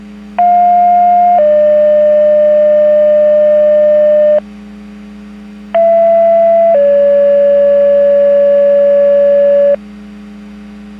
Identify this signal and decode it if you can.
Two-tone page, heard as a one-second tone, then a three-second tone, about 710 then 600 Hz, about 670 then 570 Hz. It is heard from 0.4 to 9.7 s.